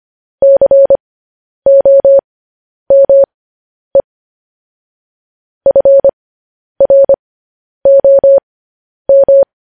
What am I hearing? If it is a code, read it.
Morse code, heard as a keyed tone, about 25 wpm, spaced as for 10 wpm, 550 Hz. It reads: COME FROM
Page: news-headlines-in-morse-code-at-10-wpm